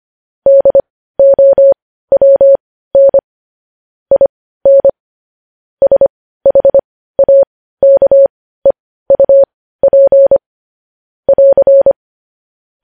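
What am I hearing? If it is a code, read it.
Morse code, heard as a keyed tone, about 25 wpm, spaced as for 15 wpm, 550 Hz. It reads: DOWN IN SHAKEUP <AR>
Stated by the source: Morse code news headlines sent at 15, preceded by vvv.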